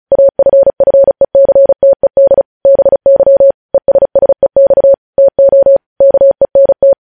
Morse code, heard as a keyed tone, about 36 wpm, 550 Hz. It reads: AFFECTED BY ESSEX TO KENT
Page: News headlines in Morse code at 35 wpm